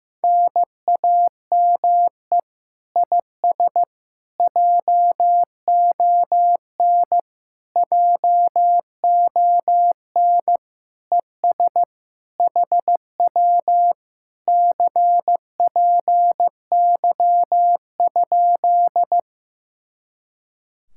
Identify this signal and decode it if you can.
Morse code, heard as a keyed tone, about 15 wpm, 700 Hz. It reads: NAME IS JON JON ES HW CPY?
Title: Speed-Racing Common QSO Phrases Encore - 15wpm